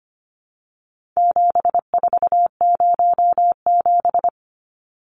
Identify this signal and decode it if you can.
Morse code, heard as a keyed tone, about 25 wpm, 700 Hz. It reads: 7407
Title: Numbers - 4-Digit 25wpm